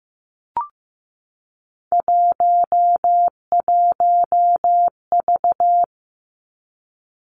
Morse code, heard as a keyed tone, about 15 wpm, 700 Hz. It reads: 11V
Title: Three Challenging Letter-Number Combinations 15wpm